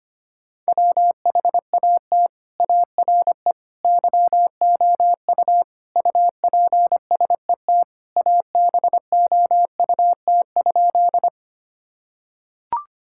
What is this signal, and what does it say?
Morse code, heard as a keyed tone, about 25 wpm, 700 Hz. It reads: WHAT ARE YOU UPSET ABOUT?
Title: Sentences from Top 1000 Words - Review Part 5 - 25wpm